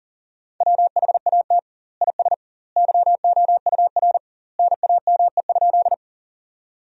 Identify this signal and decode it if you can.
Morse code, heard as a keyed tone, about 40 wpm, 700 Hz. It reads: WHAT IS YOUR NAME?
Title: 2x Word-Spacing - Sentences from Top 200 Words - 40wpm